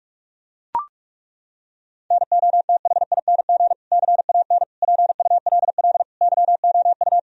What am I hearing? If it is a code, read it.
Morse code, heard as a keyed tone, about 41 wpm, 700 Hz. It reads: NOTHING CAN PULL YOU
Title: Sentences from Top 1000 Words - Review Part 2 - 45wpm